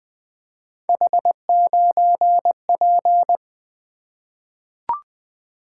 Morse code, heard as a keyed tone, about 21 wpm, 700 Hz. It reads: H9P
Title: Three Letter-Number Combinations - 20wpm